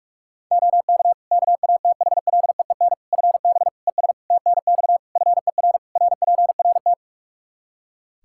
Morse code, heard as a keyed tone, about 46 wpm, 700 Hz. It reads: OK KATHLEEN FB ES TNX FER RPRT
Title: Practice QSO - Exchange 2 - Info - 45wpm